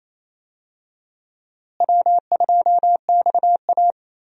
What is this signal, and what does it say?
Morse code, heard as a keyed tone, about 28 wpm, 700 Hz. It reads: W2XA
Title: USA Call Signs - 4 Characters - Repeated 2x - 28wpm